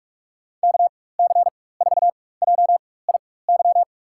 Morse code, heard as a keyed tone, about 46 wpm, spaced as for 21 wpm, 700 Hz. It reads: KC4JIY
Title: USA Call Signs - 4x Character Spacing - 45wpm